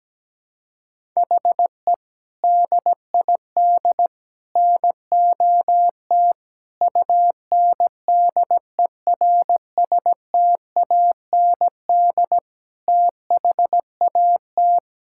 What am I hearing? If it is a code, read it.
Morse code, heard as a keyed tone, about 17 wpm, 700 Hz. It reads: HE DID NOT UNDERSTAND THAT
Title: Sentences from Top 1000 Words - Review Part 5 - 17wpm